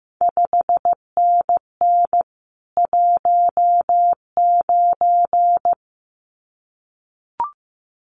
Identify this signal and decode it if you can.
Morse code, heard as a keyed tone, about 15 wpm, 700 Hz. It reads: 5NN 19
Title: CQ WW DX Contest Exchange Encore - 15wpm